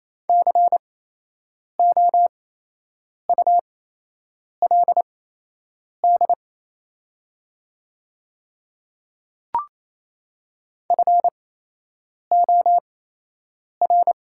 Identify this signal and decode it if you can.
Morse code, heard as a keyed tone, about 27 wpm, spaced as for 8 wpm, 700 Hz. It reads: COULD FOR
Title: 8x Character Spacing - Top 100 Words - 28wpm